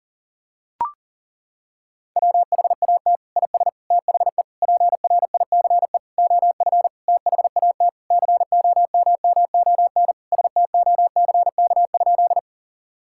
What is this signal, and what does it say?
Morse code, heard as a keyed tone, about 37 wpm, 700 Hz. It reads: WHAT IS THE PRICE OF THAT COMMON STOCK?
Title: Sentences from Top 1000 Words - Review Part 5 - 40wpm